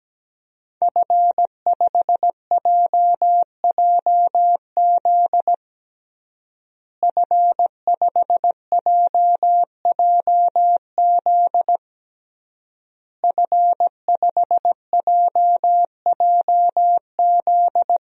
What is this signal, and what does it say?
Morse code, heard as a keyed tone, about 17 wpm, 700 Hz. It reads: F5JJZ F5JJZ F5JJZ
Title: DXCC-100 Call Signs - 5 Characters - Repeated 3x - 17wpm